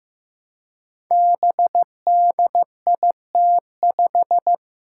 Morse code, heard as a keyed tone, about 15 wpm, 700 Hz. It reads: BDIT5
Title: Five Challenging Letter-Number Combinations 15wpm